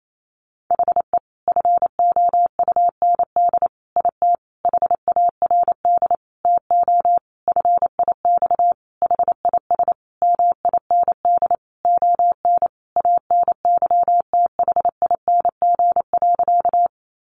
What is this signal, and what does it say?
Morse code, heard as a keyed tone, about 28 wpm, 700 Hz. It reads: HE FOUND IT HARD TO FIX HIS MIND ON ANYTHING.
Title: Sentences from Top 1400 Words 28wpm